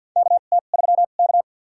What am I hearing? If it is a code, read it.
Morse code, heard as a keyed tone, about 50 wpm, spaced as for 36 wpm, 700 Hz. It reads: KT3K
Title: USA Call Signs - 4 Characters - 2x Character Spacing - 50wpm